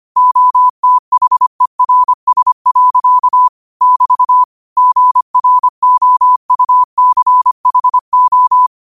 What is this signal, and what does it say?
Morse code, heard as a keyed tone, about 25 wpm, 1000 Hz. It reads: OTHERS. <BT> GROUCHO
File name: qotdfiles / 25wpm